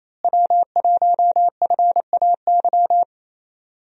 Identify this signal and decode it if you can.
Morse code, heard as a keyed tone, about 27 wpm, 700 Hz. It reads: W1FAY
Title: USA Call Signs - 5 Characters - Repeated 2x - 28wpm